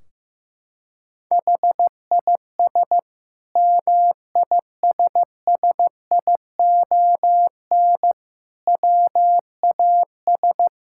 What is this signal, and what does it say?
Morse code, heard as a keyed tone, about 15 wpm, 700 Hz. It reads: HIS MISSION WAS
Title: Sentences from Top 1600 Words 15wpm